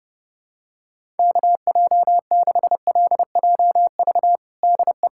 Morse code, heard as a keyed tone, about 32 wpm, 700 Hz. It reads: KJ6LJV DE
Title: USA Callsign DE Callsign - 6 Characters - 30wpm